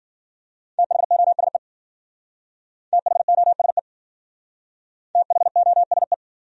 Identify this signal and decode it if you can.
Morse code, heard as a keyed tone, about 51 wpm, 700 Hz. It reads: THOSE THOSE THOSE